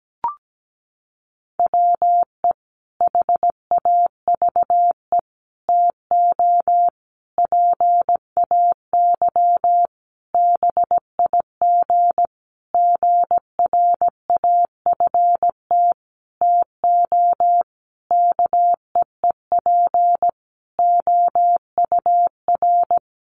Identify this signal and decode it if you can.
Morse code, heard as a keyed tone, about 17 wpm, 700 Hz. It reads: WE HAVE TO PAY BIG GRAFT TO KEEP OUR
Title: A Sound Of Thunder 17wpm